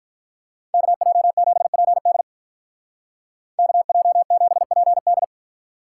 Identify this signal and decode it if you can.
Morse code, heard as a keyed tone, about 46 wpm, 700 Hz. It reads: KJ7PD KJ7PD